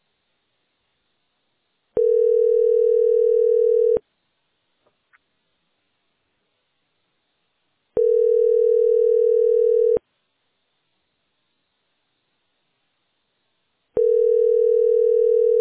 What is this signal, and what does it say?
Ringback tone (North America), 440 + 480 Hz.